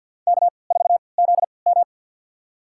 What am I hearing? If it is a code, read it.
Morse code, heard as a keyed tone, about 52 wpm, spaced as for 28 wpm, 700 Hz. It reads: K4ZM
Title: USA Call Signs - 4 Characters - 3x Character Spacing - 50wpm